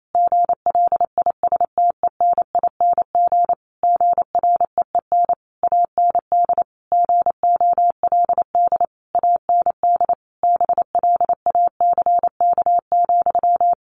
Morse code, heard as a keyed tone, about 28 wpm, 700 Hz. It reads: GLISTENING GREEN AND GOLD AND BLACK,
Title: A Sound Of Thunder 28wpm